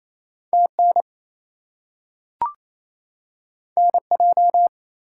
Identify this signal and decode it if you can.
Morse code, heard as a keyed tone, about 28 wpm, 700 Hz. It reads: TN NJ
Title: US States and Canadian Provinces 28wpm